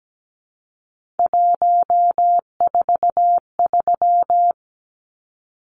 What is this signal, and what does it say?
Morse code, heard as a keyed tone, about 17 wpm, 700 Hz. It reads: 143
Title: Numbers - 3-Digit 17wpm